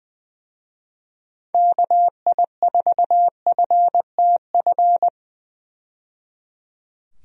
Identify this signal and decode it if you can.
Morse code, heard as a keyed tone, about 20 wpm, 700 Hz. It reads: KI4FTF
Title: USA Call Signs - 6 Characters - 20wpm